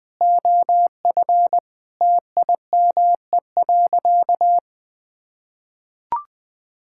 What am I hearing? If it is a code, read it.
Morse code, heard as a keyed tone, about 19 wpm, 700 Hz. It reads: OF TIME.
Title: Sentences from Top 1000 Words - Review Part 5 - 20wpm